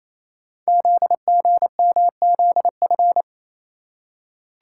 Morse code, heard as a keyed tone, about 28 wpm, 700 Hz. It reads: ZGMZF
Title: Code Groups - 5 Letters - 28wpm